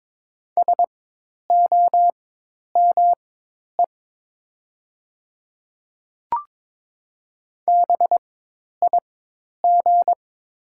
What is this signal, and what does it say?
Morse code, heard as a keyed tone, about 22 wpm, spaced as for 10 wpm, 700 Hz. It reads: SOME BIG
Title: 4x Character Spacing - Top 300 Words - 22wpm